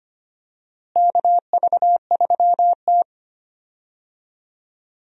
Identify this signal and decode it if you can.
Morse code, heard as a keyed tone, about 25 wpm, 700 Hz. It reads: KV3T
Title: USA Call Signs - 4 Characters - Repeated 2x - 25wpm